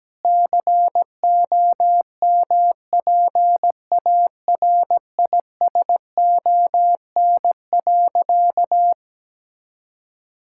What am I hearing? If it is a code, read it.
Morse code, heard as a keyed tone, about 17 wpm, 700 Hz. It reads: COMPARISON.
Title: Sentences from Top 1600 Words 17wpm